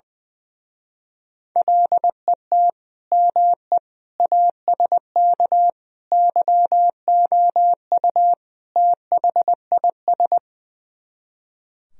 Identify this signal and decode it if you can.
Morse code, heard as a keyed tone, about 20 wpm, 700 Hz. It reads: LET ME ASK YOU THIS
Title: Speed-Racing - Sets of 5 Words 20wpm